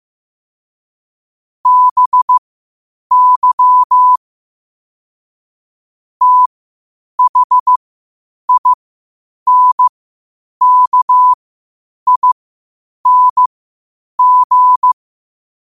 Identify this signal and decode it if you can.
Morse code, heard as a keyed tone, about 15 wpm, spaced as for 8 wpm, 1000 Hz. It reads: BY THINKING
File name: 8wpm